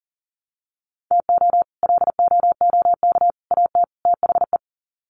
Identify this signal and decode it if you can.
Morse code, heard as a keyed tone, about 40 wpm, 700 Hz. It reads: TO LOOK AT THE
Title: Sets of 4 Words 40wpm